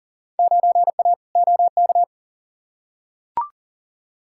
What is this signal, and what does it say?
Morse code, heard as a keyed tone, about 40 wpm, 700 Hz. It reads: 9A OK
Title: Field Day Exchanges Encore - 40wpm